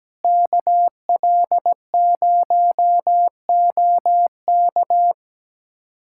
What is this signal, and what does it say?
Morse code, heard as a keyed tone, about 17 wpm, 700 Hz. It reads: KL0OK
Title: USA Callsign DE Callsign - 5 Characters - 17wpm